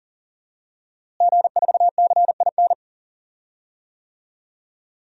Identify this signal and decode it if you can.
Morse code, heard as a keyed tone, about 40 wpm, 700 Hz. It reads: G4CIN
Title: DXCC-100 Call Signs - 5 Characters - Repeated 2x - 40wpm